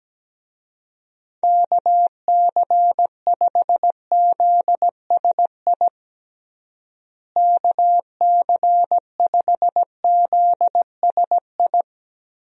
Morse code, heard as a keyed tone, about 17 wpm, 700 Hz. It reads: KC5ZSI KC5ZSI
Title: USA Call Signs - 6 Characters - Repeated 2x - 17wpm